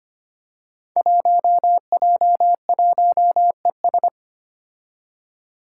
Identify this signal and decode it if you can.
Morse code, heard as a keyed tone, about 25 wpm, 700 Hz. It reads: 1J1ES
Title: Five Challenging Letter-Number Combinations 25wpm